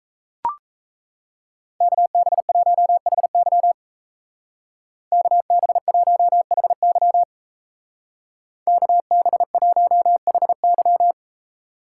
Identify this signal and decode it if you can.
Morse code, heard as a keyed tone, about 35 wpm, 700 Hz. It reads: KB1HY KB1HY KB1HY
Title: Speed Racing - USA 2x2 Call Signs 28wpm